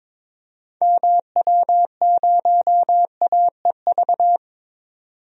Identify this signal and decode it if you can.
Morse code, heard as a keyed tone, about 21 wpm, 700 Hz. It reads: MW0AEV
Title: DXCC-100 Call Signs - 6 Characters - 22wpm